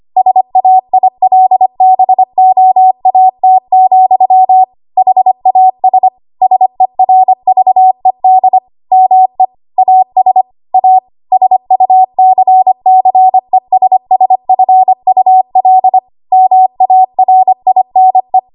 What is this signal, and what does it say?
Morse code, heard as a keyed tone, about 25 wpm, 750 Hz. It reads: SAILBOAT, HAS SERVED ME AS A SUCCESSFUL MARINE
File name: Archive / 25 WPM / 25WPM